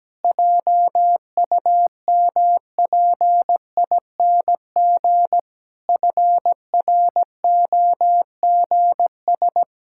Morse code, heard as a keyed tone, about 17 wpm, 700 Hz. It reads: JUMPING FROGS
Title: Pangram Challenge v2 - 17wpm